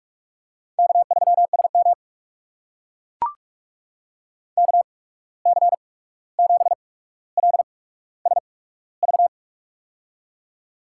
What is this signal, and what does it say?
Morse code, heard as a keyed tone, about 44 wpm, spaced as for 17 wpm, 700 Hz. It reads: K3SM KC7LSV
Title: USA Call Signs - 8x Character Spacing - 45wpm